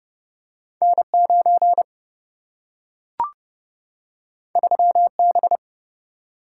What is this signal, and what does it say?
Morse code, heard as a keyed tone, about 30 wpm, 700 Hz. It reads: N 9 3 B
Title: Single-Letter Single-Number 30wpm